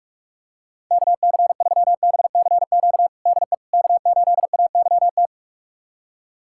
Morse code, heard as a keyed tone, about 47 wpm, 700 Hz. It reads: KC3BCQ DE K8AYT
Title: USA Callsign DE Callsign 45wpm